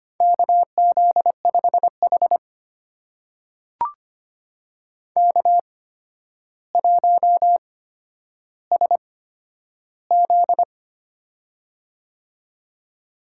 Morse code, heard as a keyed tone, about 25 wpm, spaced as for 11 wpm, 700 Hz. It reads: KZ5H K1SZ
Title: USA Call Signs - 4 Characters - 8x Character Spacing - 25wpm